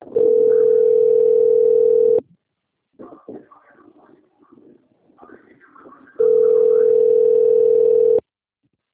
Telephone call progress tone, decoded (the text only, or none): ringback tone